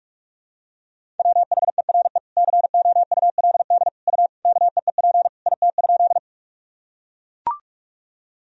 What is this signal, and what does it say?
Morse code, heard as a keyed tone, about 41 wpm, 700 Hz. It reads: WHERE COULD U KEEP IT?
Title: Sentences from Top 300 Words and Abbreviations 45wpm